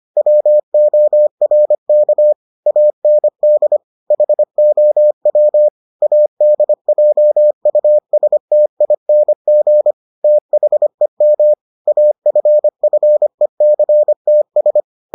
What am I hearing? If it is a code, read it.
Morse code, heard as a keyed tone, about 25 wpm, 600 Hz. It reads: WORK AND HOW ADJUSTING THEM AFFECTS